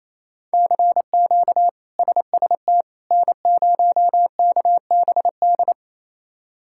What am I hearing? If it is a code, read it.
Morse code, heard as a keyed tone, about 28 wpm, 700 Hz. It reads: CQ SST N0KBD
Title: K1USN SST QSO Practice - 28wpm